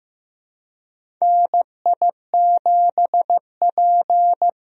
Morse code, heard as a keyed tone, about 15 wpm, 700 Hz. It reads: NI7P
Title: USA Call Signs - 4 Characters - 15wpm